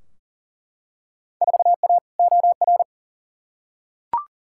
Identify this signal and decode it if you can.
Morse code, heard as a keyed tone, about 38 wpm, 700 Hz. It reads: 4A OR
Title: Field Day Exchanges Encore - 40wpm